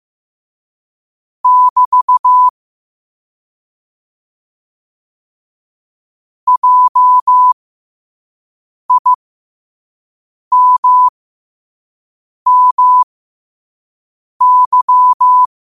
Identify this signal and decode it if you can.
Morse code, heard as a keyed tone, about 15 wpm, spaced as for 5 wpm, 1000 Hz. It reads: <BT> JIMMY
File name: qotdfiles / 5wpm